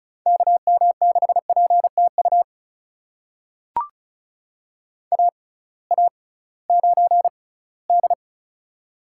Morse code, heard as a keyed tone, about 36 wpm, spaced as for 20 wpm, 700 Hz. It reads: KM6PTU AA9D
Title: USA Call Signs - 6x Character Spacing - 35wpm